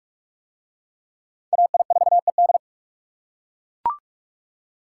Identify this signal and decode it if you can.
Morse code, heard as a keyed tone, about 41 wpm, 700 Hz. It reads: AI4ED